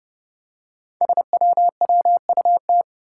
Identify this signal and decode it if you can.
Morse code, heard as a keyed tone, about 30 wpm, 700 Hz. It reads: SWWUT
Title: Five Challenging Letter-Number Combinations 30wpm